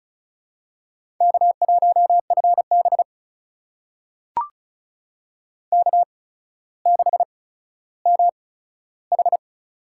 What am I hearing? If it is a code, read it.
Morse code, heard as a keyed tone, about 36 wpm, spaced as for 15 wpm, 700 Hz. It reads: K1FB K6MH
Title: USA Call Signs - 4 Characters - 8x Character Spacing - 35wpm